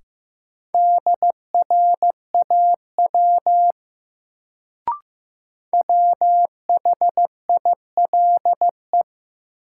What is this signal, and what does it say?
Morse code, heard as a keyed tone, about 15 wpm, 700 Hz. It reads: DRAW WHILE
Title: Sentences from Top 300 Words - New Word Drill - 15wpm